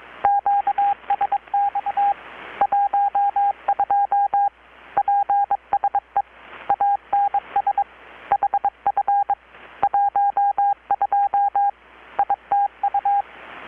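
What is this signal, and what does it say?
Morse code, heard as a keyed tone, about 22 wpm, 810 Hz, with radio noise. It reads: QSX 12 PSE ANS HF 12 ITU
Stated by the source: Morse from KPH's Night of Nights XX maritime radiotelegraphy event, received in Chiapas, Mexico about 3,800 km away on a Kenwood R-600 with a grounded 10 m long-wire antenna.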